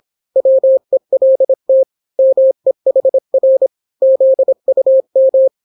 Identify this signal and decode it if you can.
Morse code, heard as a keyed tone, about 26 wpm, 530 Hz. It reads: WELT MEHR ZUM